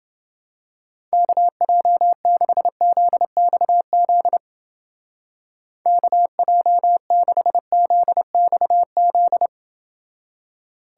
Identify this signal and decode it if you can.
Morse code, heard as a keyed tone, about 29 wpm, 700 Hz. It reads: KJ6ZXZ KJ6ZXZ